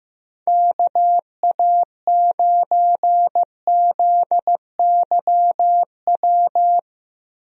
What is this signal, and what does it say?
Morse code, heard as a keyed tone, about 15 wpm, 700 Hz. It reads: KA9ZYW
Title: USA Call Signs - 6 Characters - Repeated 3x - 15wpm